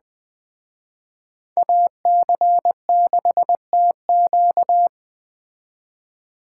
Morse code, heard as a keyed tone, about 20 wpm, 700 Hz. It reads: AC6TQ
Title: Speed Racing - USA Call Signs 17wpm